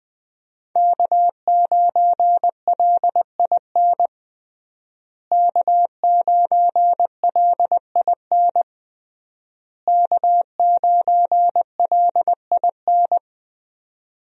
Morse code, heard as a keyed tone, about 20 wpm, 700 Hz. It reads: K9LIN K9LIN K9LIN
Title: USA Call Signs - 5 Characters - Repeated 3x - 20wpm